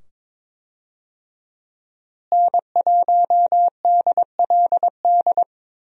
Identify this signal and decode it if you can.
Morse code, heard as a keyed tone, about 22 wpm, 700 Hz. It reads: N1DLD
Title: USA Call Signs - 5 Characters - Repeated 2x - 22wpm